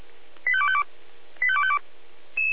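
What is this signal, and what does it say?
Five-tone selective call (ZVEI), heard as quick runs of five tones, tones of about 70 ms. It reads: 85252 85252
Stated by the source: ZVEI five-tone selective call 85259 received by SDR.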